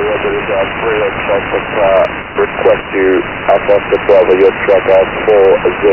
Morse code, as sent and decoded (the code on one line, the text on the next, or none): none